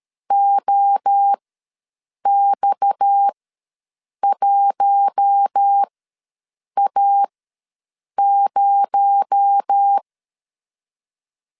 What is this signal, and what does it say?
Morse code, heard as a keyed tone, about 13 wpm, spaced as for 7 wpm, 800 Hz. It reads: OX1A0
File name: LEZIONI / 6 PKX0 / 6 PKX0 ws 12wpm 3448